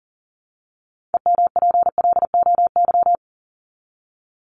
Morse code, heard as a keyed tone, about 37 wpm, 700 Hz. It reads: EMPLOY
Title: ED Suffix - Top 100 Words with Base Words 40wpm